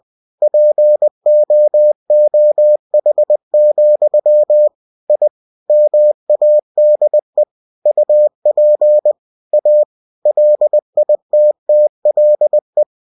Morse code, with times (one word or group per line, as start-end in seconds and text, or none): 0.4-4.7 POOH,
5.1-5.3 I
5.7-7.4 MADE
7.9-9.1 UP
9.5-9.8 A
10.3-12.8 LITTLE